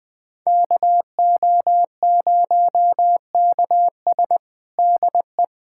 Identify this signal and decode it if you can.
Morse code, heard as a keyed tone, about 21 wpm, 700 Hz. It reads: KO0KS DE
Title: USA Callsign DE Callsign 20wpm